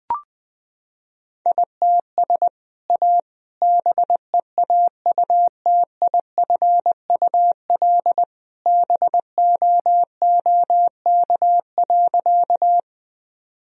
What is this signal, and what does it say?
Morse code, heard as a keyed tone, about 21 wpm, 700 Hz. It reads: ITS A BEAUTIFUL BOOK.
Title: Sentences from Top 1000 Words - Review Part 1 - 20wpm